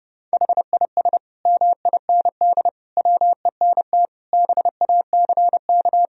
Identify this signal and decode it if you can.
Morse code, heard as a keyed tone, about 30 wpm, 700 Hz. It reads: HIS MIND WENT BACK
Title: Sentences from Top 1000 Words - Review Part 3 - 30wpm